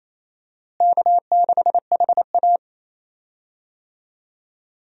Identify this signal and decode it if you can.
Morse code, heard as a keyed tone, about 28 wpm, 700 Hz. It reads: K6HA
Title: USA Call Signs - 4 Characters - Repeated 3x - 28wpm